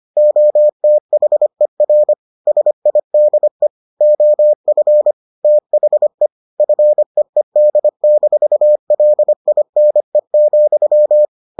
Morse code, heard as a keyed tone, about 25 wpm, 600 Hz. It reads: OTHER SIDE OF THE FEED-LINE,